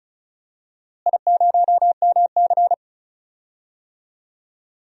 Morse code, heard as a keyed tone, about 36 wpm, 690 Hz. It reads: I0MC
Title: DXCC-100 Call Signs - 4 Characters - Repeated 3x - 35wpm